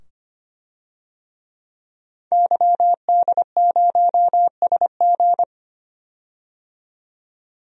Morse code, heard as a keyed tone, about 25 wpm, 700 Hz. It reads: YD0SG
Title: DXCC-100 Call Signs - 5 Characters - 25wpm